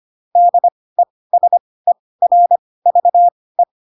Morse code, heard as a keyed tone, about 25 wpm, spaced as for 18 wpm, 700 Hz. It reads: DESERVE
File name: Pinocchio ch3 12 25 ik1hge